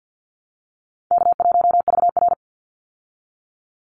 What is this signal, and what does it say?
Morse code, heard as a keyed tone, about 50 wpm, 700 Hz. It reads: K1VR
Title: Call Signs Repeated 2x - 50wpm